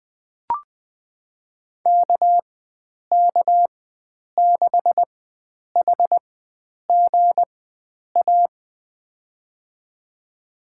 Morse code, heard as a keyed tone, about 20 wpm, spaced as for 9 wpm, 700 Hz. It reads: KK6HGA
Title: USA Call Signs - 6 Characters - 4x Character Spacing - 20wpm